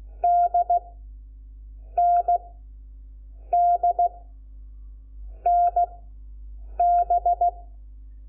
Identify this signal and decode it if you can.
Morse code, heard as a keyed tone, about 16 wpm, spaced as for 6 wpm, 700 Hz. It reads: DND NB
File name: Lecciones / CLASE Nº5